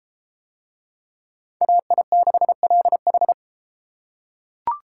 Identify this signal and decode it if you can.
Morse code, heard as a keyed tone, about 32 wpm, 700 Hz. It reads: AI6LH